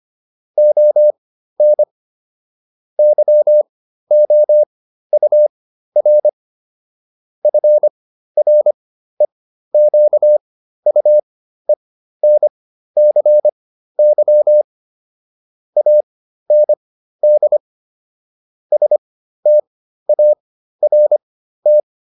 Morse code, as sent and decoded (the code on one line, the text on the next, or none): --- -. / -.-- --- ..- .-. / ..-. .-. --.- ..- . -. -.-. -.-- / .- -. -.. / ... - .- .-. -
ON YOUR FREQUENCY AND START